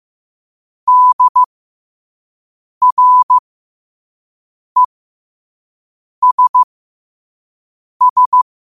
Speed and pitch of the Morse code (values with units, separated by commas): 15 wpm, spaced as for 5 wpm, 1000 Hz